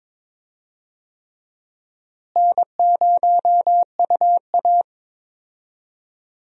Morse code, heard as a keyed tone, about 22 wpm, 700 Hz. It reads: N0UA
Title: USA Call Signs - 4 Characters - Repeated 3x - 22wpm